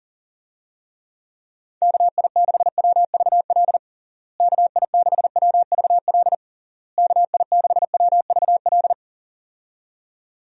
Morse code, heard as a keyed tone, about 40 wpm, 700 Hz. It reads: KI6WVL KI6WVL KI6WVL
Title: USA Call Signs - 6 Characters - Repeated 3x - 40wpm